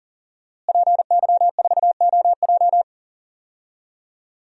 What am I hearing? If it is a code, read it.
Morse code, heard as a keyed tone, about 40 wpm, 700 Hz. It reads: PY4OJ
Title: DXCC-100 Call Signs - 5 Characters - Repeated 3x - 40wpm